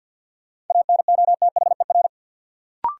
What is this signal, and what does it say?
Morse code, heard as a keyed tone, about 48 wpm, 700 Hz. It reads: ANOTHER
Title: Sentences from Top 600 Words - New Word Drill - 50wpm